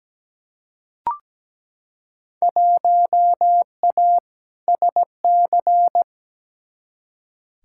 Morse code, heard as a keyed tone, about 17 wpm, 700 Hz. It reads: 1A SC
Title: Field Day Exchanges - 17wpm